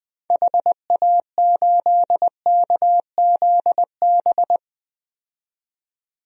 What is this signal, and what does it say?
Morse code, heard as a keyed tone, about 21 wpm, 700 Hz. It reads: HA8KZB
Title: DXCC-100 Call Signs - 6 Characters - 20wpm